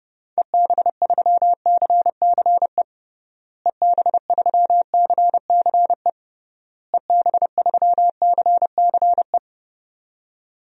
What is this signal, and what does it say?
Morse code, heard as a keyed tone, about 30 wpm, 700 Hz. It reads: EB3CCE EB3CCE EB3CCE
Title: DXCC-100 Call Signs - 6 Characters - Repeated 3x - 30wpm